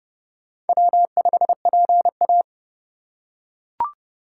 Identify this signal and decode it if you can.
Morse code, heard as a keyed tone, about 28 wpm, 700 Hz. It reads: W5PA